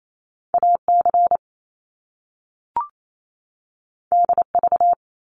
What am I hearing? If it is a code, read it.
Morse code, heard as a keyed tone, about 28 wpm, 700 Hz. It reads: AC DV
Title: Top Two Letter Combinations in Words 28wpm